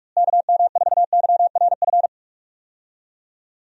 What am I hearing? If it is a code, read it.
Morse code, heard as a keyed tone, about 46 wpm, 700 Hz. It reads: KM4YRF
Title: USA Call Signs - 6 Characters - Repeated 3x - 45wpm